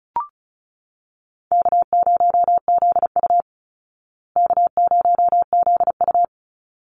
Morse code, heard as a keyed tone, about 34 wpm, 700 Hz. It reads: K0ZU K0ZU
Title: Easy Call Signs Repeated 2x - 35wpm